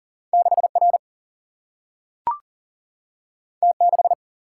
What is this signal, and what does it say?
Morse code, heard as a keyed tone, about 40 wpm, 700 Hz. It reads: 6R T6
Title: Single-Letter Single-Number 40wpm